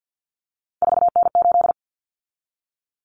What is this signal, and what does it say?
Morse code, heard as a keyed tone, about 50 wpm, 700 Hz. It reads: 4N8
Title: Common Cut Numbers - 3-Digit 50wpm